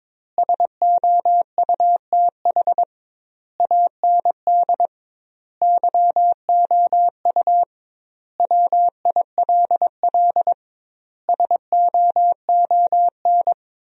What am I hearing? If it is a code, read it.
Morse code, heard as a keyed tone, about 21 wpm, 700 Hz. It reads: SOUTH AND YOU WILL SOON
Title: 2x Word-Spacing - Sentences from Top 400 Words - 22wpm